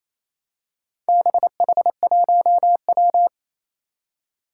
Morse code, heard as a keyed tone, about 27 wpm, 700 Hz. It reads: BH1W
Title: Four Challenging Letter-Number Combinations 28wpm